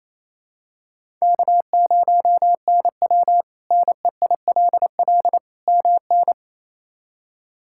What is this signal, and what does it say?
Morse code, heard as a keyed tone, about 28 wpm, 700 Hz. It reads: K0NW NEILL MN
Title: K1USN SST QSO Practice - 28wpm